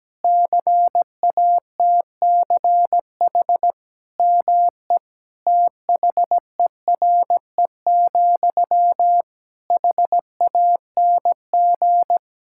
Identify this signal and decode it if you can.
Morse code, heard as a keyed tone, about 17 wpm, 700 Hz. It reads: CATCH ME THERE, HANG
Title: Sentences from Top 1600 Words 17wpm